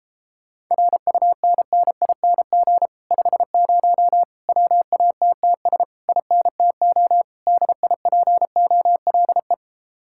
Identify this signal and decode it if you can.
Morse code, heard as a keyed tone, about 38 wpm, 700 Hz. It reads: RUNNING 50 WATTS INTO DIPOLE